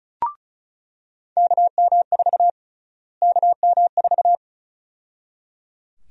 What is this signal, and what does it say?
Morse code, heard as a keyed tone, about 34 wpm, 700 Hz. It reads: KM4 KM4
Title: USA Call Sign Prefixes - Repeated 2x - 35wpm